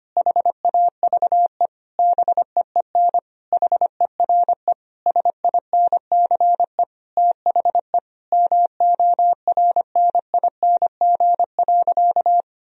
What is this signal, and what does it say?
Morse code, heard as a keyed tone, about 25 wpm, 700 Hz. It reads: HAVE BEEN HERE SINCE THE MORNING.
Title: Sentences from Top 1000 Words - Review Part 1 - 25wpm